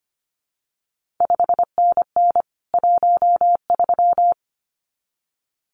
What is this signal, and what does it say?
Morse code, heard as a keyed tone, about 25 wpm, 700 Hz. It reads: 5NN 13
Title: CQ WW DX Contest Exchange Encore - 25wpm